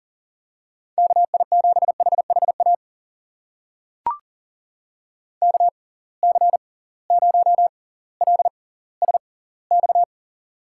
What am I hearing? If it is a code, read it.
Morse code, heard as a keyed tone, about 40 wpm, spaced as for 20 wpm, 700 Hz. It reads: KI7HHA KC0LSX
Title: USA Call Signs - 6 Characters - 6x Character Spacing - 40wpm